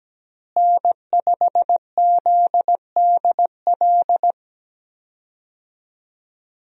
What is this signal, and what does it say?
Morse code, heard as a keyed tone, about 17 wpm, 700 Hz. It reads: N5ZDL
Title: USA Call Signs - 5 Characters - Repeated 3x - 17wpm